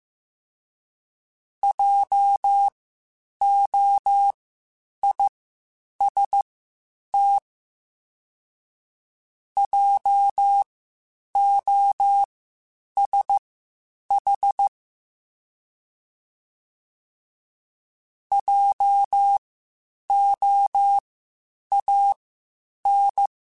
Morse code, heard as a keyed tone, about 15 wpm, spaced as for 8 wpm, 800 Hz. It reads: JOIST JOSH JOAN
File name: Spaced Alphas / Spaced Alphas mp3s / Js sp @13